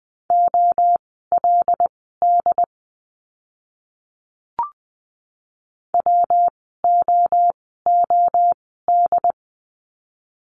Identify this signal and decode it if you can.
Morse code, heard as a keyed tone, about 20 wpm, spaced as for 14 wpm, 700 Hz. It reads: OLD WOOD
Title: 115 Lesson 19f - Letter w - Words 20wpm